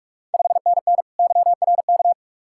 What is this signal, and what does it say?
Morse code, heard as a keyed tone, about 46 wpm, 690 Hz. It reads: 5NN YRK
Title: Ontario QSO Party Exchange 45wpm